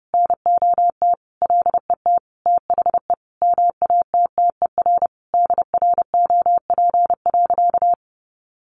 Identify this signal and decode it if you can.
Morse code, heard as a keyed tone, about 30 wpm, 700 Hz. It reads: NOT LET THE MATTER DROP.
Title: Sentences from Top 1400 Words 30wpm